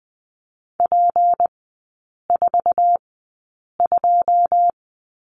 Morse code, heard as a keyed tone, about 20 wpm, spaced as for 8 wpm, 690 Hz. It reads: P42
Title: 159 Lesson 26e - Letter g - Three Characters 20wpm